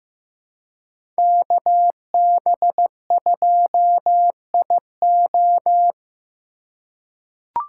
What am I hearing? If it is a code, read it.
Morse code, heard as a keyed tone, about 15 wpm, 700 Hz. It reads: KB2IO
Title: USA Call Signs - 5 Characters - 2x Character Spacing - 15wpm